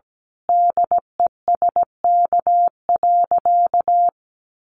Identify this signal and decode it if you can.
Morse code, heard as a keyed tone, about 17 wpm, 700 Hz. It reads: DESK.
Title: A Sound Of Thunder 17wpm